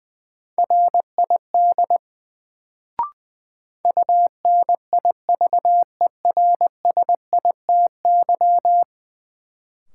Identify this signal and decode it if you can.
Morse code, heard as a keyed tone, about 20 wpm, 700 Hz. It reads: RID UNIVERSITY
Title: Sets of 4 Words - New Word Drill - 20wpm